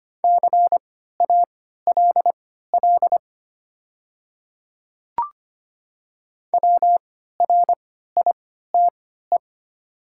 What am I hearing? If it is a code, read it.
Morse code, heard as a keyed tone, about 25 wpm, spaced as for 14 wpm, 700 Hz. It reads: CALL WRITE
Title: 3x Character Spacing - Top 100 Words - 25wpm